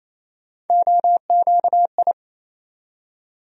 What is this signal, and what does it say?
Morse code, heard as a keyed tone, about 27 wpm, 700 Hz. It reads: OQI
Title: USA Call Sign Postfixes - 3 Letters 28wpm